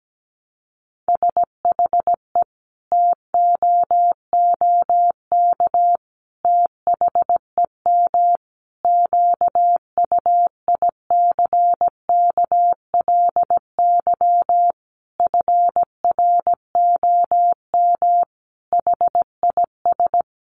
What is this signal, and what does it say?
Morse code, heard as a keyed tone, about 17 wpm, 700 Hz. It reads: SHE TOOK THEM QUICKLY FROM HIS
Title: Sentences from Top 1300 Words 17wpm